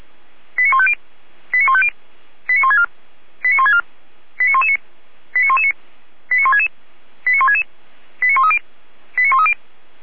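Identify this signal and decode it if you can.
Five-tone selective call (ZVEI), heard as quick runs of five tones, tones of about 70 ms. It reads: 89150 89150 89164 89164 89119 89119 89166 89166 89130 89130
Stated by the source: Received on an SDR as ZVEI five-tone call, address 89084.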